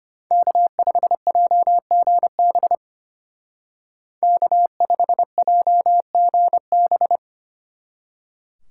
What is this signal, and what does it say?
Morse code, heard as a keyed tone, about 31 wpm, 700 Hz. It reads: K5JGB K5JGB
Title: Speed Racing - USA 1x3 Call Signs 25wpm